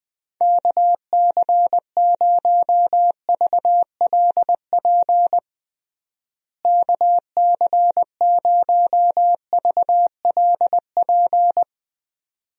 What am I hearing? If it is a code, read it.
Morse code, heard as a keyed tone, about 20 wpm, 700 Hz. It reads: KC0VLP KC0VLP
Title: USA Call Signs - 6 Characters - Repeated 2x - 20wpm